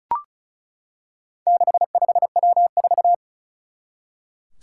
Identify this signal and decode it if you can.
Morse code, heard as a keyed tone, about 36 wpm, 700 Hz. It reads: 65W4
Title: Four Challenging Letter-Number Combinations 35wpm